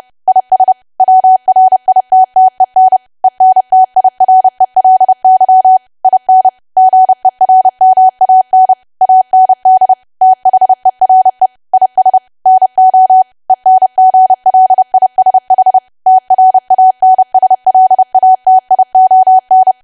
Morse code, heard as a keyed tone, about 30 wpm, 750 Hz. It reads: IS WRITTEN ENTIRELY IN GERMAN AND THERE IS NO ENGLISH TRANSLATION
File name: cwa / W1AW-Level3 / 30WPM